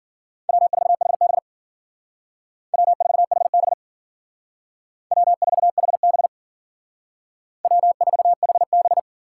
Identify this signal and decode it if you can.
Morse code, heard as a keyed tone, about 52 wpm, 700 Hz. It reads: W4HB W4HB W4HB W4HB